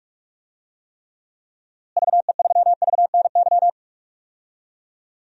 Morse code, heard as a keyed tone, about 46 wpm, 700 Hz. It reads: VE3VNY
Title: DXCC-100 Call Signs - 6 Characters - Repeated 3x - 45wpm